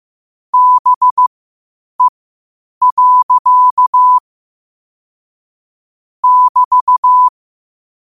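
Morse code, heard as a keyed tone, about 15 wpm, spaced as for 8 wpm, 1000 Hz. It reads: BE. <BT>
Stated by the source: Morse code quotes at 8 WPM.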